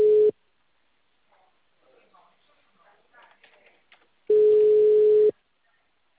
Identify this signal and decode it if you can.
Ringback tone (Europe, 425 Hz), with speech.
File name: out- -2004- - - .6896